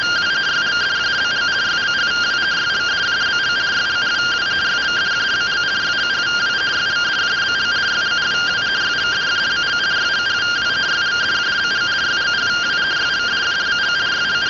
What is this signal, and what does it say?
Two-tone teleprinter signal, about 170 Hz shift.